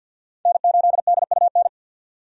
Morse code, heard as a keyed tone, about 52 wpm, 700 Hz. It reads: N8DUN
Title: USA Call Signs - 5 Characters - 50wpm